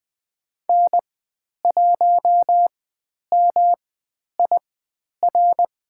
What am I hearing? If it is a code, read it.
Morse code, heard as a keyed tone, about 20 wpm, spaced as for 10 wpm, 700 Hz. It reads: N1MIR
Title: 109 Lesson 18g - Letter m - Call Signs 20wpm